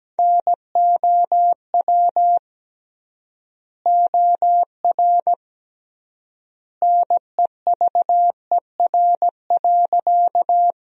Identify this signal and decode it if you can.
Morse code, heard as a keyed tone, about 17 wpm, 700 Hz. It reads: NOW OR NEVER.
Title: 3x Word-Spacing - Sentences from Top 300 Words - 17wpm